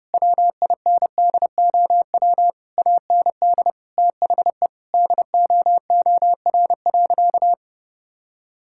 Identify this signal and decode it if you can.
Morse code, heard as a keyed tone, about 30 wpm, 700 Hz. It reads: WINDOW AND THE DOOR.
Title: Sentences from Top 1600 Words 30wpm